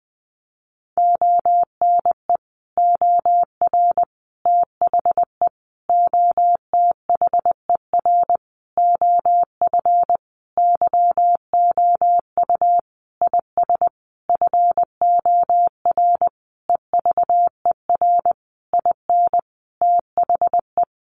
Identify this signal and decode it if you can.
Morse code, heard as a keyed tone, about 20 wpm, 700 Hz. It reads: ONE OR THE OTHER OF YOU IS FOR EVER IN THE
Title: Sentences from Top 400 Words 20wpm